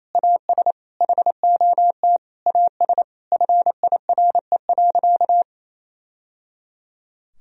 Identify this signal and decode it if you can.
Morse code, heard as a keyed tone, about 28 wpm, 700 Hz. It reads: AS HOT AS FIRE.
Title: Sentences from Top 1000 Words - Review Part 3 - 28wpm